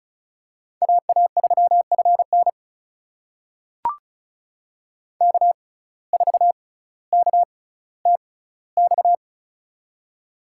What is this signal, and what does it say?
Morse code, heard as a keyed tone, about 36 wpm, spaced as for 18 wpm, 700 Hz. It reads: AA3FN K4KTX
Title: USA Call Signs - 5 Characters - 6x Character Spacing - 35wpm